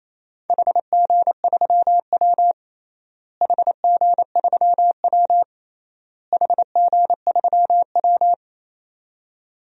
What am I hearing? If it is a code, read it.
Morse code, heard as a keyed tone, about 28 wpm, 700 Hz. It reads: HG3W HG3W HG3W
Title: DXCC-100 Call Signs - 4 Characters - Repeated 3x - 28wpm